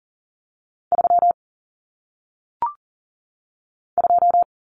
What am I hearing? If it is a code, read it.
Morse code, heard as a keyed tone, about 40 wpm, 700 Hz. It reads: 3 2
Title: Common Cut Numbers - 1-Digit 40wpm